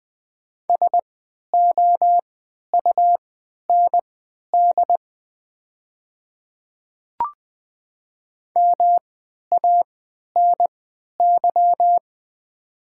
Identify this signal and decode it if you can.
Morse code, heard as a keyed tone, about 20 wpm, spaced as for 11 wpm, 700 Hz. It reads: SOUND MANY
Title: 3x Character Spacing - Top 100 Words - 20wpm